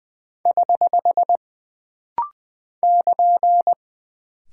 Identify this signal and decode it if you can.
Morse code, heard as a keyed tone, about 20 wpm, 700 Hz. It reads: <HH> <KN>
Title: Single Punctuation-Prosign - Rapid-Fire - 20wpm